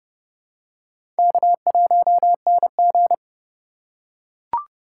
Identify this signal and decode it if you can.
Morse code, heard as a keyed tone, about 30 wpm, 700 Hz. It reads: K1NG
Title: USA Call Signs - 4 Characters - 2x Character Spacing - 30wpm